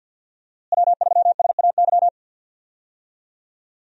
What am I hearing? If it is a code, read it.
Morse code, heard as a keyed tone, about 47 wpm, 700 Hz. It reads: W3SAY
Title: USA Call Signs - 5 Characters - Repeated 2x - 50wpm